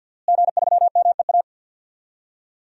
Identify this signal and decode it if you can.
Morse code, heard as a keyed tone, about 52 wpm, 700 Hz. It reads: G3MEA
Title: DXCC-100 Call Signs - 5 Characters - Repeated 2x - 50wpm